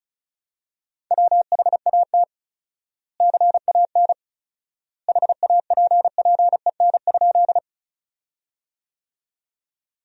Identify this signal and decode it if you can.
Morse code, heard as a keyed tone, about 34 wpm, 700 Hz. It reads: WHAT CAN HAPPEN?
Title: 4x Word-Spacing - Sentences from Top 400 Words - 35wpm